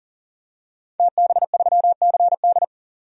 Morse code, heard as a keyed tone, about 39 wpm, 700 Hz. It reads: TB3CD
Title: DXCC-100 Call Signs - 5 Characters - 40wpm